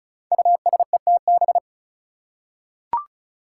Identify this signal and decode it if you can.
Morse code, heard as a keyed tone, about 36 wpm, 700 Hz. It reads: USETB
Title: Five Challenging Letter-Number Combinations 35wpm